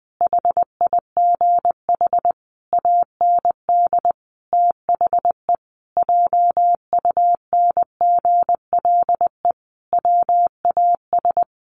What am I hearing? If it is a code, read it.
Morse code, heard as a keyed tone, about 20 wpm, 700 Hz. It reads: HIGH AND THE JUNGLE WAS
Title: A Sound Of Thunder 20wpm